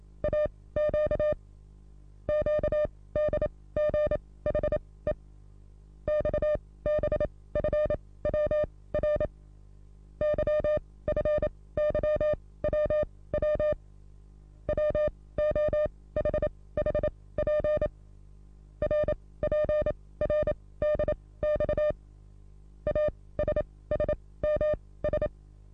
Morse code, heard as a keyed tone, about 27 wpm, spaced as for 18 wpm, 600 Hz. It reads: AQ QDGHE XBFWR YFYWW WOHHP RPRDX ASSMS
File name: repos / Morsekurs / DJ4UF Morsen27